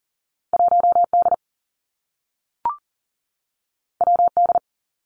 Morse code, heard as a keyed tone, about 42 wpm, 700 Hz. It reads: 1 D W D
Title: Two Challenging Letter-Number Combinations 40wpm